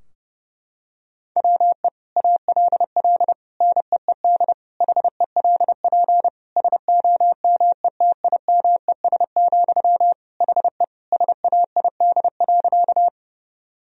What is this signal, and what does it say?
Morse code, heard as a keyed tone, about 30 wpm, 700 Hz. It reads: WE ALL NEED HELP SOMETIMES, HE SAID.
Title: Sentences from Top 1000 Words - Review Part 3 - 30wpm